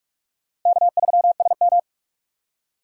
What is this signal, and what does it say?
Morse code, heard as a keyed tone, about 46 wpm, 700 Hz. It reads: K3SM
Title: USA Call Signs - 4 Characters - 4x Character Spacing - 45wpm